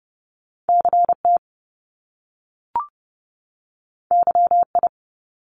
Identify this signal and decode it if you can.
Morse code, heard as a keyed tone, about 30 wpm, 700 Hz. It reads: CT YI
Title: Top Two Letter Combinations in Words 30wpm